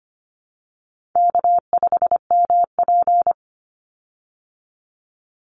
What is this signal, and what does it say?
Morse code, heard as a keyed tone, about 25 wpm, 700 Hz. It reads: K5MP
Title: Easy Call Signs Repeated 2x - 25wpm